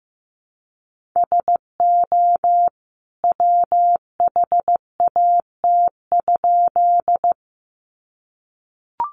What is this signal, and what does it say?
Morse code, heard as a keyed tone, about 15 wpm, 700 Hz. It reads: SO WHAT?
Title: A Sound Of Thunder 15wpm